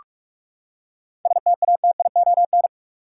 Morse code, heard as a keyed tone, about 46 wpm, 700 Hz. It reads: STATION
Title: Seven Letter Words - 45wpm